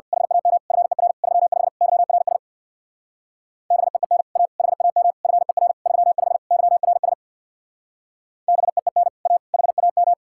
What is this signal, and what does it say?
Morse code, heard as a keyed tone, about 57 wpm, 700 Hz. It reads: HAM FER 35 YRS BEEN A HAM FER 35 YRS BEEN A HAM
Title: Speed-Racing Common QSO Phrases - 50wpm